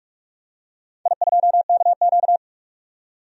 Morse code, heard as a keyed tone, about 43 wpm, 700 Hz. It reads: I2KQ
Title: DXCC-100 Call Signs - 4 Characters - 45wpm